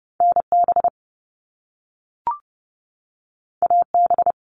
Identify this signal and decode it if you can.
Morse code, heard as a keyed tone, about 30 wpm, 700 Hz. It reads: NB AB
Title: Canadian Provinces and Territories 30wpm